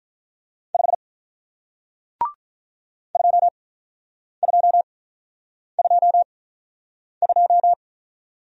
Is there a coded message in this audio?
Yes: Morse code from 0.7 to 7.7 s, heard as a keyed tone, about 48 wpm, spaced as for 9 wpm, 700 Hz. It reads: 5 2222